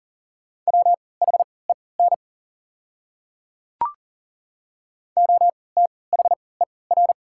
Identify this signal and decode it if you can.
Morse code, heard as a keyed tone, about 40 wpm, spaced as for 23 wpm, 700 Hz. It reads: WHEN OTHER
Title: 3x Character Spacing - Top 300 Words - 40wpm